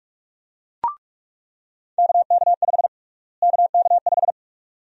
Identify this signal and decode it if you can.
Morse code, heard as a keyed tone, about 47 wpm, 700 Hz. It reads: KK5 KK5
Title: USA Call Sign Prefixes - Repeated 2x - 45wpm